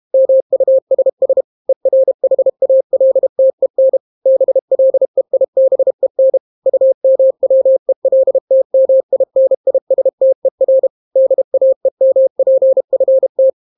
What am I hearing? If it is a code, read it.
Morse code, heard as a keyed tone, about 32 wpm, 530 Hz. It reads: MUSS ERHALTEN BLEIBEN UMWELTMINISTER DAEMPFT